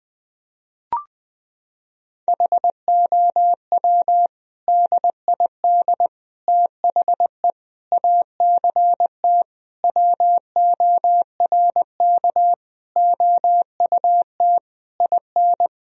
Morse code, heard as a keyed tone, about 20 wpm, 700 Hz. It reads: HOW DID THE ACT WORK OUT IN
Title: Sentences from Top 1000 Words - Review Part 4 - 20wpm